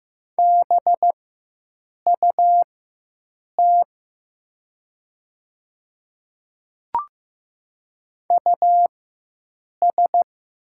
Morse code, heard as a keyed tone, about 15 wpm, spaced as for 7 wpm, 700 Hz. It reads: BUT US
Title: 4x Character Spacing - Top 200 Words - 15wpm